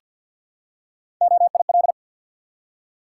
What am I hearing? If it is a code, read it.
Morse code, heard as a keyed tone, about 53 wpm, 700 Hz. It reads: OIL
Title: Three Letter Words - 50wpm